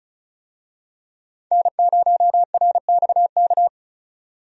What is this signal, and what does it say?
Morse code, heard as a keyed tone, about 36 wpm, 700 Hz. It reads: N0RXK
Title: USA Call Signs - 5 Characters - 35wpm